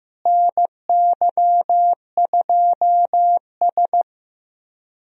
Morse code, heard as a keyed tone, about 15 wpm, 700 Hz. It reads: NY2S
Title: USA Call Signs - 4 Characters - Repeated 2x - 15wpm